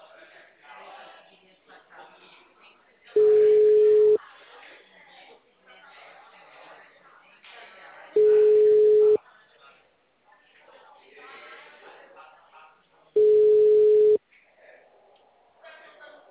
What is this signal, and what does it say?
Ringback tone (Europe, 425 Hz), with speech.
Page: Atibaia - SP 1 (CP Cartao)